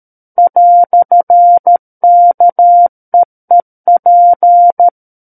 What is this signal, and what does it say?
Morse code, heard as a keyed tone, about 13 wpm, 700 Hz. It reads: "KEEP
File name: Book One / 04-Chapter 4 - 13 WPM